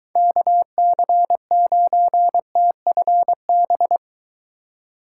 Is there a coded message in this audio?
Yes: Morse code from 0.2 to 4.0 s, heard as a keyed tone, about 24 wpm, 700 Hz. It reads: KC9TFB